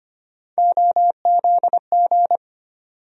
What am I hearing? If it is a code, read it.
Morse code, heard as a keyed tone, about 25 wpm, 700 Hz. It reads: OZG
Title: USA Call Sign Postfixes - 3 Letters 25wpm